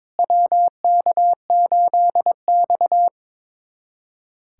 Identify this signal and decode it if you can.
Morse code, heard as a keyed tone, about 21 wpm, 700 Hz. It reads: WK8X
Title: USA Callsign DE Callsign - 4 Characters - 22wpm